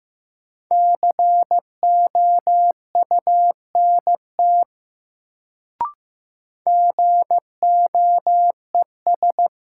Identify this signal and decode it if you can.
Morse code, heard as a keyed tone, about 15 wpm, 700 Hz. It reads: COUNT GOES
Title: Sentences from Top 1500 Words - New Word Drill Alt - 15wpm